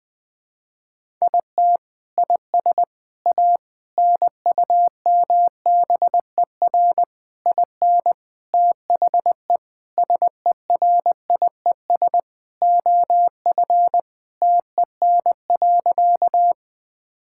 Morse code, heard as a keyed tone, about 20 wpm, 700 Hz. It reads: IT IS A NUMBER IN THE SERIES OF TEN.
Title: Sentences from Top 1000 Words - Review Part 5 - 20wpm